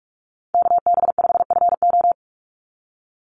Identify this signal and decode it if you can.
Morse code, heard as a keyed tone, about 44 wpm, 700 Hz. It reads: KB5FO
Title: USA Call Signs 45wpm